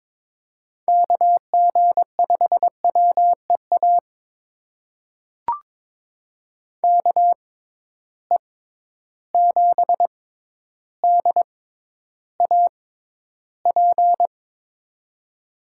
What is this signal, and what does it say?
Morse code, heard as a keyed tone, about 22 wpm, spaced as for 11 wpm, 700 Hz. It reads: KG5WEA KE7DAP
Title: USA Call Signs - 6 Characters - 6x Character Spacing - 22wpm